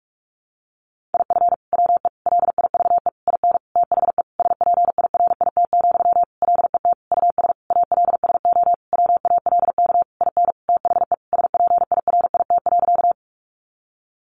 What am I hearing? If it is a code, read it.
Morse code, heard as a keyed tone, about 44 wpm, 700 Hz. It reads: IF WE LIVE IN THE SPIRIT, LET US ALSO WALK IN THE SPIRIT.
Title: Sentences from Top 1400 Words 45wpm